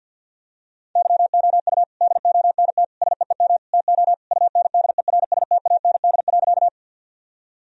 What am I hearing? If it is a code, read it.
Morse code, heard as a keyed tone, about 46 wpm, 700 Hz. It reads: YOU DONT SEEM TO UNDERSTAND.
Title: Sentences from Top 1000 Words - Review Part 1 - 50wpm